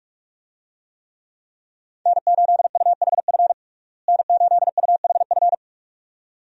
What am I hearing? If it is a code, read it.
Morse code, heard as a keyed tone, about 41 wpm, 700 Hz. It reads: N8UHF N8UHF
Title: USA Call Signs - 5 Characters - Repeated 2x - 45wpm